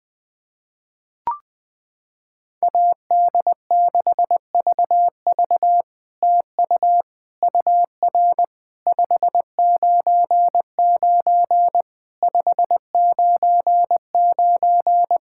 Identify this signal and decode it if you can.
Morse code, heard as a keyed tone, about 20 wpm, 700 Hz. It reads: AD6VV TU UR 599 599
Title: SOTA QSO Practice 20wpm